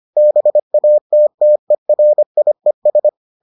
Morse code, heard as a keyed tone, about 25 wpm, 600 Hz. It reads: BATTERIES